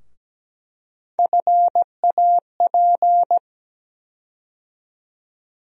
Morse code, heard as a keyed tone, about 17 wpm, 700 Hz. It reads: FAP
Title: USA Call Sign Postfixes - 3 Letters 17wpm